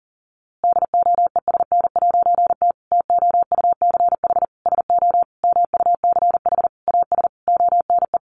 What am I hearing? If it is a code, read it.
Morse code, heard as a keyed tone, about 40 wpm, 700 Hz. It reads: DOESN'T TOUCH SO MUCH AS ONE
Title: A Sound Of Thunder 40wpm